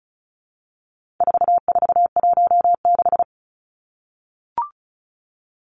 Morse code, heard as a keyed tone, about 34 wpm, 700 Hz. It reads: 4416
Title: Numbers - 4-Digit 35wpm